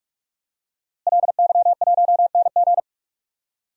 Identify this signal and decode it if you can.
Morse code, heard as a keyed tone, about 46 wpm, 700 Hz. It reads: LY1NG
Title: DXCC-100 Call Signs - 5 Characters - Repeated 3x - 45wpm